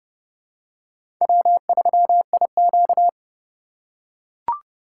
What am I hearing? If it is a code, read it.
Morse code, heard as a keyed tone, about 30 wpm, 700 Hz. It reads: W3IQ
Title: USA Call Signs - 4 Characters - 2x Character Spacing - 30wpm